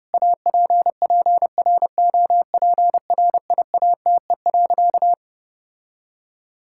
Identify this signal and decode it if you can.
Morse code, heard as a keyed tone, about 30 wpm, 700 Hz. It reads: APPROPRIATE.
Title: Sentences from Top 1600 Words 30wpm